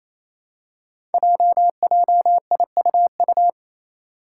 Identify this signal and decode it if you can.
Morse code, heard as a keyed tone, about 28 wpm, 700 Hz. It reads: JJIUU
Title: Five Challenging Letter-Number Combinations 28wpm